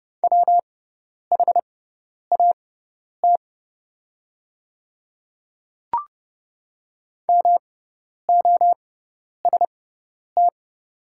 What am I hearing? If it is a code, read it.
Morse code, heard as a keyed tone, about 31 wpm, spaced as for 10 wpm, 690 Hz. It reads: WHAT MOST
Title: 6x Character Spacing - Top 200 Words - 30wpm